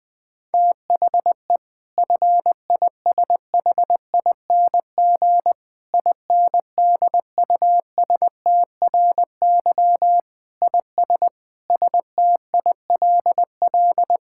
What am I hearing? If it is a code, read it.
Morse code, heard as a keyed tone, about 20 wpm, 700 Hz. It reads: THE FISHING INDUSTRY IS STILL
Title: Sentences from Top 1000 Words - Review Part 6 - 20wpm